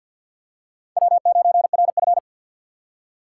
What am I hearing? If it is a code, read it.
Morse code, heard as a keyed tone, about 53 wpm, 700 Hz. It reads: W9RF
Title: USA Call Signs - 4 Characters - 6x Character Spacing - 50wpm